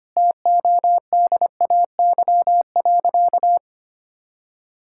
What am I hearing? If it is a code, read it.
Morse code, heard as a keyed tone, about 25 wpm, 700 Hz. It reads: TODAY.
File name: Easy Sentences with Hints - 25wpm